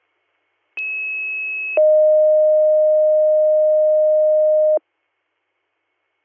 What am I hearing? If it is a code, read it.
Two-tone page, heard as a one-second tone, then a three-second tone, about 2600 then 620 Hz. It is heard from 0.8 to 4.8 s.